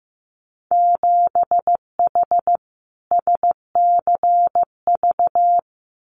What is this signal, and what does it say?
Morse code, heard as a keyed tone, about 15 wpm, 700 Hz. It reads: 7H SCV
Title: Winter Field Day Exchanges Encore 15wpm